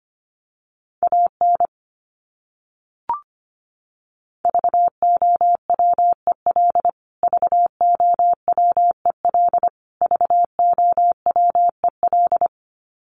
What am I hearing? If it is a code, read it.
Morse code, heard as a keyed tone, about 25 wpm, 700 Hz. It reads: AN VOWEL VOWEL VOWEL
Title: Top 500 Words Repeated 25wpm